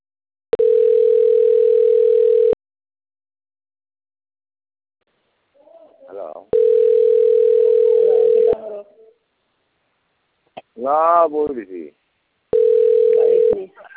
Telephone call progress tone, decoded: ringback tone